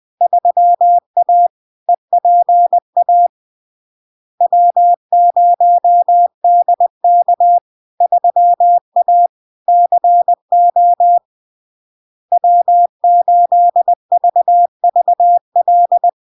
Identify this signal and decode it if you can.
Morse code, heard as a keyed tone, about 20 wpm, 700 Hz. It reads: 3A EPA W0DK 3A CO W8VVL
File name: LARK-FD-2022-Log-in-CW-20wpm-700-Hz